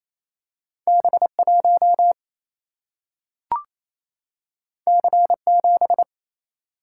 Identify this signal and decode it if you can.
Morse code, heard as a keyed tone, about 28 wpm, 700 Hz. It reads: B 1 C 7
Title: Single-Letter Single-Number 28wpm